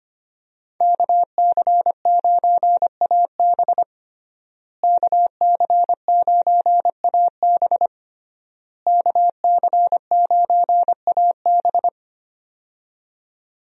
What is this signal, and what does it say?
Morse code, heard as a keyed tone, about 25 wpm, 700 Hz. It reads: KC9AB KC9AB KC9AB
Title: USA Call Signs - 5 Characters - Repeated 3x - 25wpm